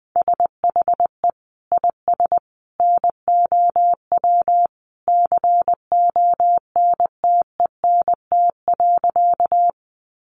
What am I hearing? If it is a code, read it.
Morse code, heard as a keyed tone, about 20 wpm, 700 Hz. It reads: SHE IS NOW CONTENT.
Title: Sentences from Top 900 Words 20wpm